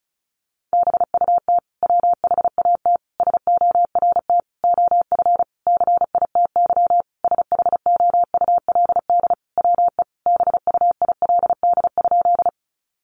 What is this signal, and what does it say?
Morse code, heard as a keyed tone, about 34 wpm, 700 Hz. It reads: BUT WHAT SORT OF CITY SHOULD WE BUILD?
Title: Sentences from Top 1300 Words 35wpm